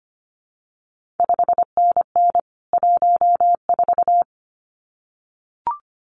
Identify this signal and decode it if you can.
Morse code, heard as a keyed tone, about 25 wpm, 700 Hz. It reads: 5NN 14
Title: CQ WW DX Contest Exchange - 25wpm